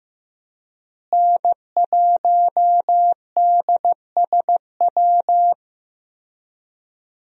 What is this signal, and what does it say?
Morse code, heard as a keyed tone, about 15 wpm, 700 Hz. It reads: N1DSW
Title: USA Call Signs - 5 Characters - Repeated 3x - 15wpm